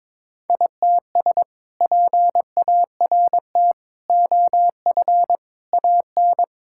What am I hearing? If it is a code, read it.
Morse code, heard as a keyed tone, about 22 wpm, 700 Hz. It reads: ITS PART OF AN
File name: practice / 22wpm / Sentences with 2-Word Phrases - 22wpm